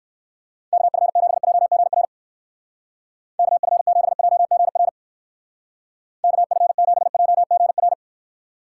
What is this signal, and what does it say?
Morse code, heard as a keyed tone, about 59 wpm, 700 Hz. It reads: KF7JGR KF7JGR KF7JGR